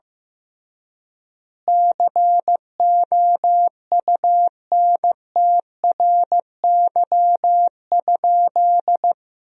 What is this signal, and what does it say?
Morse code, heard as a keyed tone, about 15 wpm, 700 Hz. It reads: COUNTRY?
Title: 3x Word-Spacing - Sentences from Top 300 Words - 15wpm